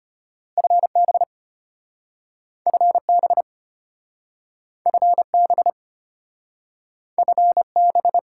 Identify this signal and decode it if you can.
Morse code, heard as a keyed tone, about 33 wpm, 700 Hz. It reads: FB FB FB FB